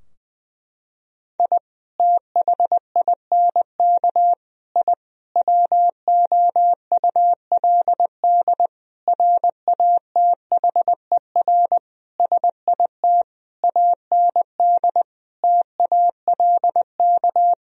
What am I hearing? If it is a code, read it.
Morse code, heard as a keyed tone, about 20 wpm, 700 Hz. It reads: I THINK I WOULD RATHER SIT AND TALK
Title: Sentences from Top 1000 Words - Review Part 4 - 20wpm